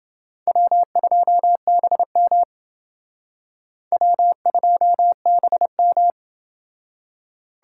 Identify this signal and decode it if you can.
Morse code, heard as a keyed tone, about 30 wpm, 700 Hz. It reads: W2BM W2BM